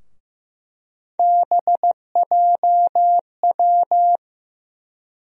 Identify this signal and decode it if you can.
Morse code, heard as a keyed tone, about 15 wpm, 700 Hz. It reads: BJW
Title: USA Call Sign Postfixes - 3 Letters 15wpm